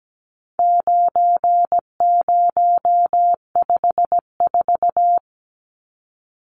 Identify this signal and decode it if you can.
Morse code, heard as a keyed tone, about 17 wpm, 700 Hz. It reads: 9054
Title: Numbers - 4-Digit 17wpm